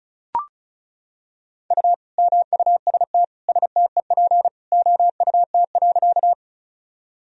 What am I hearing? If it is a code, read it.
Morse code, heard as a keyed tone, about 34 wpm, 700 Hz. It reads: U MUST STEP OUT.
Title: Sentences from Top 400 Words and Abbreviations 35wpm